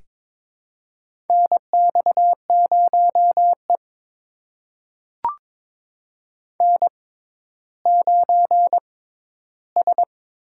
Morse code, heard as a keyed tone, about 22 wpm, spaced as for 12 wpm, 700 Hz. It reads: NX0E N9S
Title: USA Call Signs - 6x Character Spacing - 22wpm